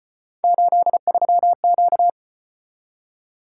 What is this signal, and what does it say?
Morse code, heard as a keyed tone, about 34 wpm, 700 Hz. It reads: 83Q